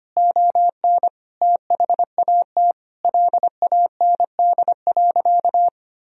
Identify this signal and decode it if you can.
Morse code, heard as a keyed tone, about 25 wpm, 700 Hz. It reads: ON THAT LAND.
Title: Sentences from Top 1000 Words - Review Part 1 - 25wpm